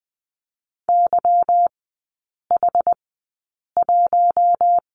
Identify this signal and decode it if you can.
Morse code, heard as a keyed tone, about 20 wpm, spaced as for 8 wpm, 690 Hz. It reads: YH1
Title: 191 Lesson 31e - Letter v - Three Characters 20wpm